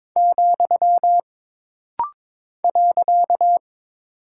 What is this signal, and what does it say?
Morse code, heard as a keyed tone, about 22 wpm, 700 Hz. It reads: , .
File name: practice / 22wpm / Single Punctuation-Prosign - Rapid-Fire - 22wpm